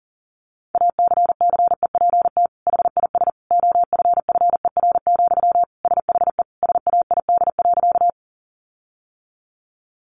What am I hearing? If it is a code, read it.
Morse code, heard as a keyed tone, about 40 wpm, 700 Hz. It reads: ACCEPT HIS OFFER, SHE SAID.
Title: Sentences from Top 1300 Words 40wpm